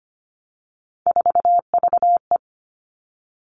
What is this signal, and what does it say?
Morse code, heard as a keyed tone, about 25 wpm, 700 Hz. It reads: 4VE
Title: Three Challenging Letter-Number Combinations 25wpm